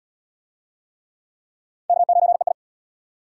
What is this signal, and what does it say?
Morse code, heard as a keyed tone, about 77 wpm, 700 Hz. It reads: K9EI